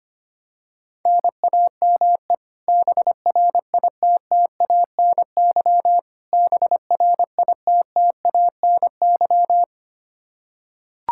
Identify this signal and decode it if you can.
Morse code, heard as a keyed tone, about 25 wpm, 700 Hz. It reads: NAME BRITTANY BRITTANY
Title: Practice QSO - Exchange 1 - Beginning - 25wpm